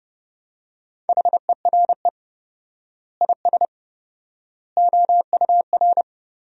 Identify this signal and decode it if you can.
Morse code, heard as a keyed tone, about 30 wpm, 700 Hz. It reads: HERE IS OUR
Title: 4x Word-Spacing - Sentences from Top 500 Words - 30wpm